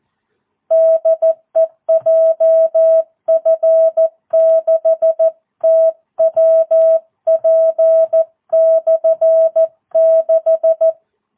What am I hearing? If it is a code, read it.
Morse code, heard as a keyed tone, about 14 wpm, 650 Hz. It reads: DE JF6TWP/6
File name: jh8cbh / 2022cw